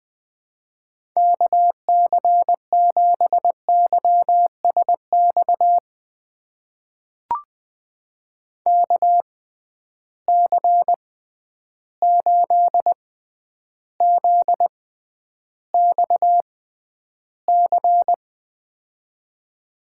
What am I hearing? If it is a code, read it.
Morse code, heard as a keyed tone, about 20 wpm, spaced as for 10 wpm, 700 Hz. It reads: KC7YSX KC8ZXC
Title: USA Call Signs - 6 Characters - 6x Character Spacing - 20wpm